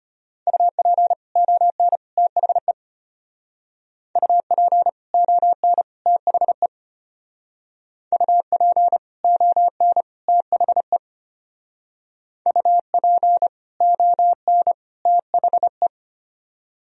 Morse code, heard as a keyed tone, about 33 wpm, 700 Hz. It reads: UP ON THE UP ON THE UP ON THE UP ON THE